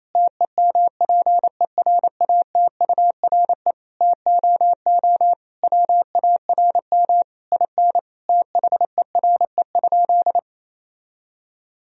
Morse code, heard as a keyed tone, about 28 wpm, 700 Hz. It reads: TEMPERATURE TOO WARM IN THERE?
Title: Sentences from Top 1000 Words - Review Part 4 - 28wpm